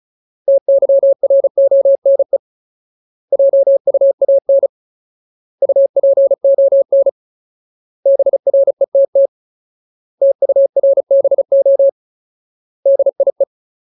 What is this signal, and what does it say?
Morse code, heard as a keyed tone, about 36 wpm, 550 Hz. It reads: TYRONE JUAN UPON BRETT TURBO DIE
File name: Ultimate Word List / 35WPM Ultimate